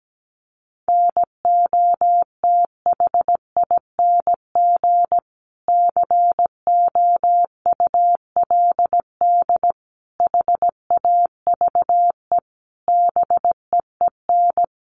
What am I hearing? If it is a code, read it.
Morse code, heard as a keyed tone, about 17 wpm, 700 Hz. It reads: NOTHING COULD HAVE BEEN
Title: Sentences from Top 500 Words 17wpm